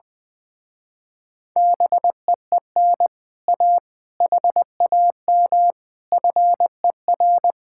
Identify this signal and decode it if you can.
Morse code, heard as a keyed tone, about 20 wpm, 700 Hz. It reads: BEEN A HAM FER
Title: Speed-Racing Common QSO Phrases - 20wpm